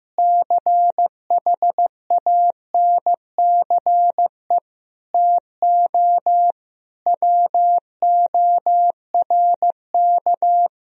Morse code, heard as a keyed tone, about 14 wpm, 700 Hz. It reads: CHANCE TO WORK
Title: Sentences from Top 1000 Words - Review Part 4 - 15wpm